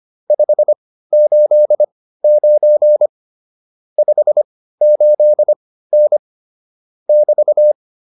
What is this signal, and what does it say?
Morse code, heard as a keyed tone, about 25 wpm, spaced as for 15 wpm, 600 Hz. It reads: 589 58N <BT>
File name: QSO110 15